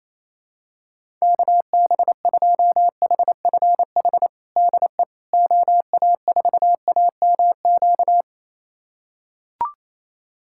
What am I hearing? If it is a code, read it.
Morse code, heard as a keyed tone, about 28 wpm, 700 Hz. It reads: KB2HFH DE OA4AMQ
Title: USA Callsign DE Callsign - 6 Characters - 28wpm